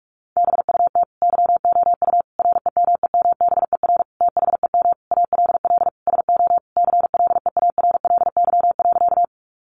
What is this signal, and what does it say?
Morse code, heard as a keyed tone, about 44 wpm, 700 Hz. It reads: BUT YOU REMEMBER THEM ALL SO CLEARLY.
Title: Sentences from Top 1300 Words 45wpm